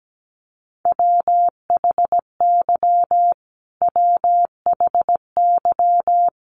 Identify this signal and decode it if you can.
Morse code, heard as a keyed tone, about 17 wpm, 700 Hz. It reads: WHY WHY
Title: Top 200 Words Repeated 17wpm